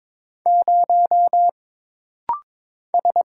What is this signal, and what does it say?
Morse code, heard as a keyed tone, about 22 wpm, 700 Hz. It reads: 0 S
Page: This rapid-fire timing is geared to push your ICR (Instant Character Recognition) skills.